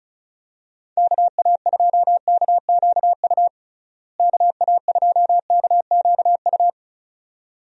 Morse code, heard as a keyed tone, about 34 wpm, 700 Hz. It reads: KA2KQU KA2KQU
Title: USA Call Signs - 6 Characters - Repeated 2x - 35wpm